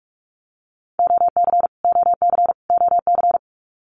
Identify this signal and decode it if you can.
Morse code, heard as a keyed tone, about 46 wpm, 700 Hz. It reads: OC OC OC
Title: Abbreviations Repeated 3x - 45wpm